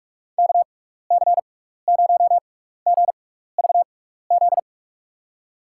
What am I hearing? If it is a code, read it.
Morse code, heard as a keyed tone, about 43 wpm, spaced as for 15 wpm, 700 Hz. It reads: KC0GVZ
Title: USA Call Signs - 6 Characters - 6x Character Spacing - 45wpm